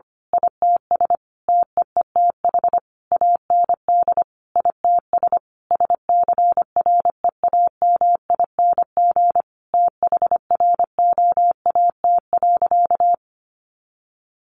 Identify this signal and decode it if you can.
Morse code, heard as a keyed tone, about 25 wpm, 700 Hz. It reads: ITS TEETH AND ITS SCREAMING THROAT.
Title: A Sound Of Thunder 25wpm